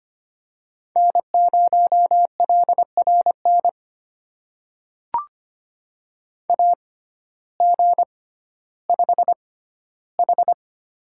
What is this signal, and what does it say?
Morse code, heard as a keyed tone, about 25 wpm, spaced as for 14 wpm, 700 Hz. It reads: N0LRN AG5H
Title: USA Call Signs - 6x Character Spacing - 25wpm